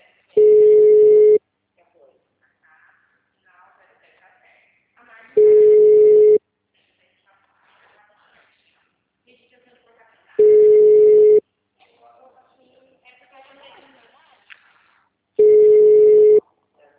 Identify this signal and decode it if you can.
Ringback tone (Europe, 425 Hz), with speech.
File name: out- -2002- - - .2152